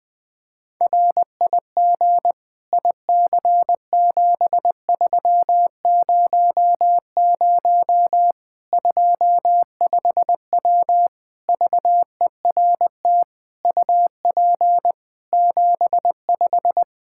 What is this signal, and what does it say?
Morse code, heard as a keyed tone, about 21 wpm, 700 Hz. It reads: RIG IC7300 25W VERT UP 75
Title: Practice QSO - Exchange 2 - Info - 20wpm